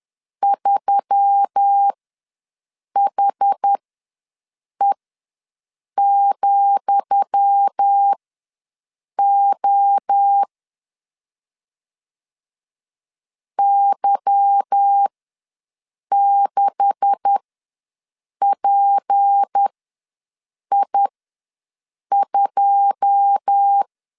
Morse code, heard as a keyed tone, about 10 wpm, spaced as for 6 wpm, 800 Hz. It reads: 3HE,O Y6PI2
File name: LEZIONI / 10 SIGNS / 10D COMA ws 10wpm 4314